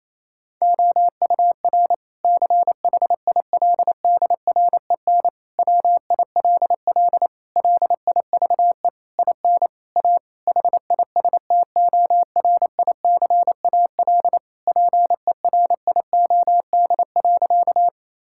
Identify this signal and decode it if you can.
Morse code, heard as a keyed tone, about 28 wpm, 700 Hz. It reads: OUR CHILDREN WILL LIVE IN A HISTORICAL PERIOD.
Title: Sentences from Top 1000 Words - Review Part 5 - 28wpm